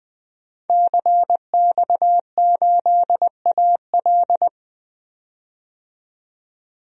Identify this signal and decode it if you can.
Morse code, heard as a keyed tone, about 20 wpm, 700 Hz. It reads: CX8AL
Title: DXCC-100 Call Signs - 5 Characters - 20wpm